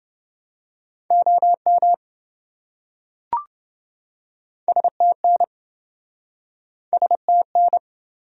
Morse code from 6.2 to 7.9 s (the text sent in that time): STN